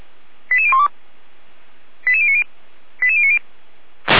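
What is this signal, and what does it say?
Five-tone selective call (ZVEI), heard as quick runs of five tones, tones of about 70 ms. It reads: 88012 88090 88090
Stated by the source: ZVEI five-tone call 88139 received on an SDR.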